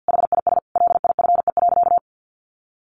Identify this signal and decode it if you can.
Morse code, heard as a keyed tone, about 50 wpm, 700 Hz. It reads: HIS LIFE.
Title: Sentences from Top 1100 Words 50wpm